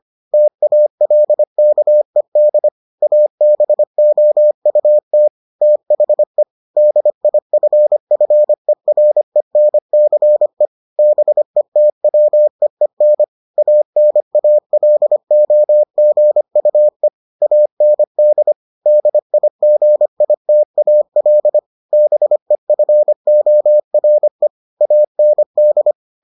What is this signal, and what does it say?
Morse code, heard as a keyed tone, about 25 wpm, 600 Hz. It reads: TALKED ABOUT THE DIFFERENCE BETWEEN ANALOGUE AND DIGITAL BEFORE AND